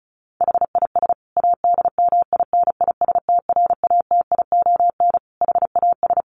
Morse code, heard as a keyed tone, about 34 wpm, 700 Hz. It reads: HIS ADMINISTRATION HAS
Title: Sentences from Top 1300 Words 35wpm